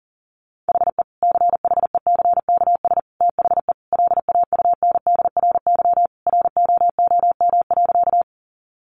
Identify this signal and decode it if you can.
Morse code, heard as a keyed tone, about 40 wpm, 700 Hz. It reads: HE CHECKS THE LAUNDRY ROOM.
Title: Easy Phrases 40wpm